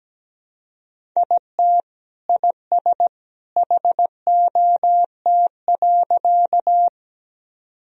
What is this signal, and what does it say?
Morse code, heard as a keyed tone, about 17 wpm, 700 Hz. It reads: IT IS HOT.
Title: Sentences from Top 1000 Words - Review Part 3 - 17wpm